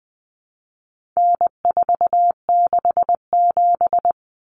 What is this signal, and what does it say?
Morse code, heard as a keyed tone, about 20 wpm, 700 Hz. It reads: N467
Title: Common Cut Numbers - 4-Digit 20wpm